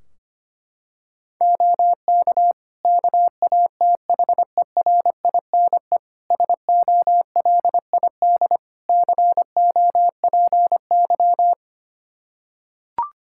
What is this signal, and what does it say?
Morse code, heard as a keyed tone, about 25 wpm, 700 Hz. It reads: OK KATHERINE SOLID COPY
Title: Practice QSO - Exchange 3 - More Info - 25wpm